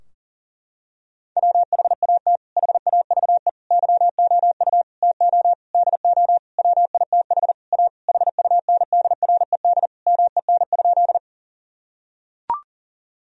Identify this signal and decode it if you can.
Morse code, heard as a keyed tone, about 40 wpm, 700 Hz. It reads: WHAT HAVE YOU TO DO WITH A HUNDRED MEN?
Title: Sentences from Top 1000 Words - Review Part 2 - 40wpm